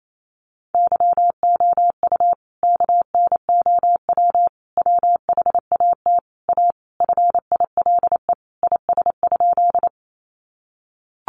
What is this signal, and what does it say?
Morse code, heard as a keyed tone, about 28 wpm, 700 Hz. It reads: YOU KNOW WHAT A FILE IS?
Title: Sentences from Top 1200 Words 28wpm